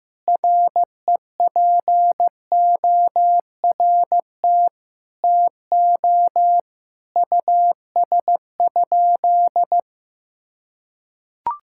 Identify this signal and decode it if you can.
Morse code, heard as a keyed tone, about 15 wpm, 700 Hz. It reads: REPORT TO US?
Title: Sentences from Top 1000 Words - Review Part 6 - 15wpm